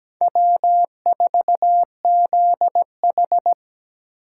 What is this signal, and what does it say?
Morse code, heard as a keyed tone, about 20 wpm, 700 Hz. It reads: W4ZH
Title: Speed Racing - USA 1x2 Call Signs 17wpm